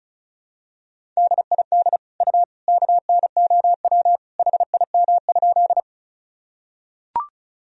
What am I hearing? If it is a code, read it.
Morse code, heard as a keyed tone, about 36 wpm, 700 Hz. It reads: DID U KNOW HIM?
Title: Sentences from Top 100 Words and Abbreviations 35wpm